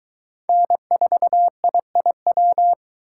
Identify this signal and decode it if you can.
Morse code, heard as a keyed tone, about 22 wpm, 700 Hz. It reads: N4IIW